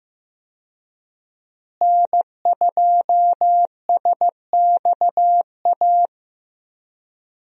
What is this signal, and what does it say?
Morse code, heard as a keyed tone, about 15 wpm, 700 Hz. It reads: N2SXA
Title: USA Call Signs - 5 Characters - Repeated 2x - 15wpm